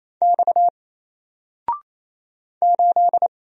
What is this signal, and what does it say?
Morse code, heard as a keyed tone, about 27 wpm, 700 Hz. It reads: X 8
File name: Single-Letter-Number-28wpm